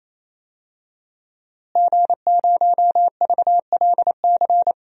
Morse code, heard as a keyed tone, about 28 wpm, 700 Hz. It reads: G0VLC
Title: DXCC-100 Call Signs - 5 Characters - Repeated 3x - 28wpm